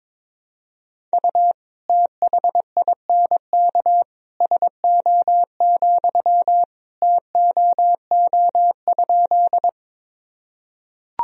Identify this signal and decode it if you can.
Morse code, heard as a keyed tone, about 22 wpm, 700 Hz. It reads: U THINK SO, TOO?
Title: Sentences from Top 200 Words and Abbreviations 22wpm